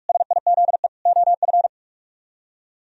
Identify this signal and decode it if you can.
Morse code, heard as a keyed tone, about 43 wpm, 700 Hz. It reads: SIZE OF